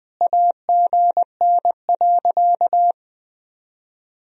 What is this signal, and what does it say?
Morse code, heard as a keyed tone, about 21 wpm, 700 Hz. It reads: AGN.
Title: Sentences from Top 300 Words and Abbreviations 20wpm